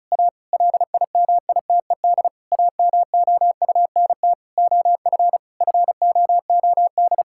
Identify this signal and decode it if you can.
Morse code, heard as a keyed tone, about 36 wpm, 700 Hz. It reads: A LIMITED AMOUNT OF FOOD
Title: Sentences from Top 1000 Words - Review Part 6 - 35wpm